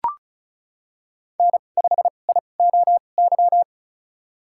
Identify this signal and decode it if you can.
Morse code, heard as a keyed tone, about 36 wpm, spaced as for 25 wpm, 700 Hz. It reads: N5IOY
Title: USA Call Signs - 5 Characters - 2x Character Spacing - 35wpm